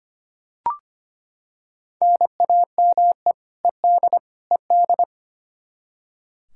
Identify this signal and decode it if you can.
Morse code, heard as a keyed tone, about 25 wpm, 700 Hz. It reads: NAME ED ED
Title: Practice QSO - Exchange 1 - Beginning - 25wpm